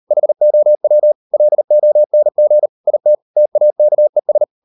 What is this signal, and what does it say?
Morse code, heard as a keyed tone, about 38 wpm, 600 Hz. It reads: HOW LONG IT TAKES